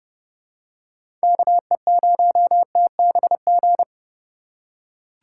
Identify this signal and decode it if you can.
Morse code, heard as a keyed tone, about 30 wpm, 700 Hz. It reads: KE0TBG
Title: USA Call Signs - 2x Character Spacing - 30wpm